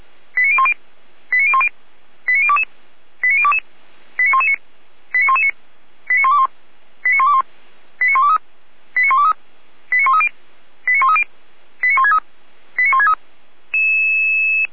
Five-tone selective call (ZVEI), heard as quick runs of five tones, tones of about 70 ms. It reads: 89010 89010 89022 89022 89119 89119 89121 89121 89123 89123 89130 89130 89163 89163